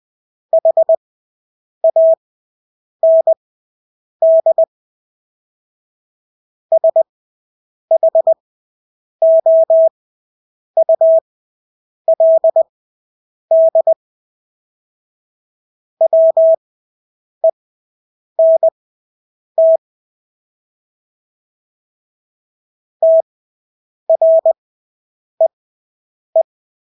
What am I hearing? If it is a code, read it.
Morse code, heard as a keyed tone, about 20 wpm, spaced as for 8 wpm, 650 Hz. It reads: HAND SHOULD WENT TREE
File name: Engelske ord 20/8 wpm